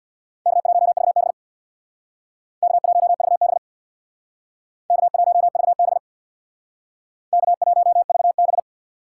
Morse code, heard as a keyed tone, about 64 wpm, 700 Hz. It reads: K1VB K1VB K1VB K1VB